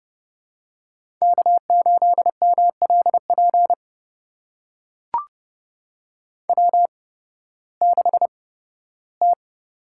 Morse code, heard as a keyed tone, about 30 wpm, spaced as for 16 wpm, 700 Hz. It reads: K8MLP W6T
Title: USA Call Signs - 5 Characters - 8x Character Spacing - 30wpm